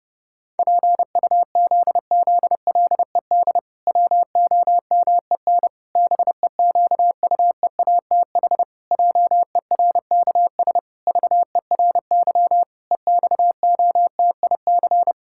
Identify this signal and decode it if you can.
Morse code, heard as a keyed tone, about 32 wpm, 700 Hz. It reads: PUZZLED WOMEN BEQUEATH JERKS VERY EXOTIC
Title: Pangram Challenge v2 - 30wpm